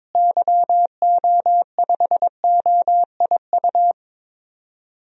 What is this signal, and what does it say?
Morse code, heard as a keyed tone, about 22 wpm, 700 Hz. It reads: YO5OIU
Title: DXCC-100 Call Signs - 6 Characters - Repeated 3x - 22wpm